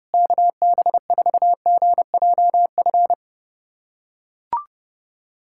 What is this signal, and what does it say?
Morse code, heard as a keyed tone, about 31 wpm, 700 Hz. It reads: KB4GJF
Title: USA Callsign DE Callsign 30wpm